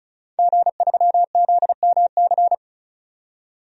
Morse code, heard as a keyed tone, about 34 wpm, 700 Hz. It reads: G3ZMC
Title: DXCC-100 Call Signs - 5 Characters - 35wpm